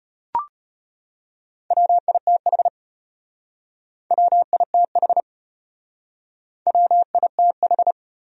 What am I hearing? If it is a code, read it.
Morse code, heard as a keyed tone, about 33 wpm, 700 Hz. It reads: WITH WITH WITH